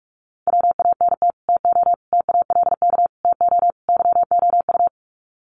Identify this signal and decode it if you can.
Morse code, heard as a keyed tone, about 44 wpm, 700 Hz. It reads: WANT TO TALK TO YOU
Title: Sets of 5 Words Encore 45wpm